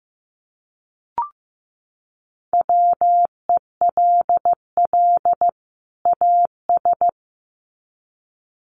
Morse code, heard as a keyed tone, about 15 wpm, 700 Hz. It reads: WELL AS
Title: Sets of 2 Words 15wpm